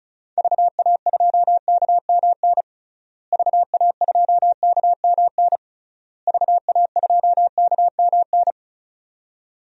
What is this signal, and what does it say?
Morse code, heard as a keyed tone, about 36 wpm, 700 Hz. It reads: VA2KMN VA2KMN VA2KMN
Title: DXCC-100 Call Signs - 6 Characters - Repeated 3x - 35wpm